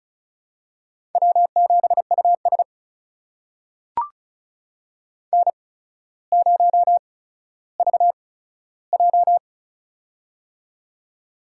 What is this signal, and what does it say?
Morse code, heard as a keyed tone, about 36 wpm, spaced as for 15 wpm, 700 Hz. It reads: W7US N0VJ
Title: USA Call Signs - 4 Characters - 8x Character Spacing - 35wpm